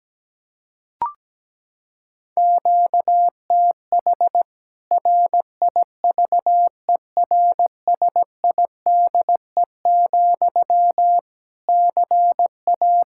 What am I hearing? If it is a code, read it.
Morse code, heard as a keyed tone, about 17 wpm, 700 Hz. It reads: QTH RIVERSIDE, CA
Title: K1USN SST QSO Practice 17wpm